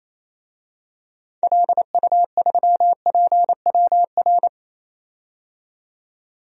Morse code, heard as a keyed tone, about 28 wpm, 700 Hz. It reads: LU3PWR
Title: DXCC-100 Call Signs - 6 Characters - 28wpm